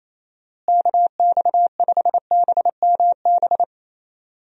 Morse code, heard as a keyed tone, about 28 wpm, 700 Hz. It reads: KX5BMB
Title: USA Call Signs - 6 Characters - Repeated 2x - 28wpm